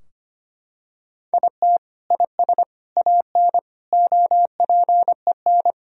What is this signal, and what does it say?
Morse code, heard as a keyed tone, about 25 wpm, 700 Hz. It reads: IT IS AN OPEN
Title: Sentences from Top 1000 Words - Review Part 3 - 25wpm